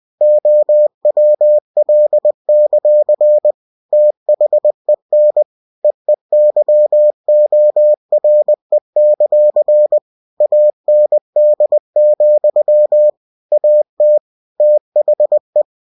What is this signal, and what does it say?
Morse code, heard as a keyed tone, about 20 wpm, 600 Hz. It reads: OWL; THEN EEYORE; AND, AT THE
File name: cw-books / winnie-the-pooh / winnie-20wpm-page-0079